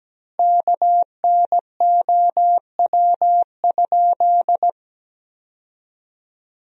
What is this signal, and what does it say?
Morse code, heard as a keyed tone, about 17 wpm, 700 Hz. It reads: KNOW?
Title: 4x Word-Spacing - Sentences from Top 100 Words - 17wpm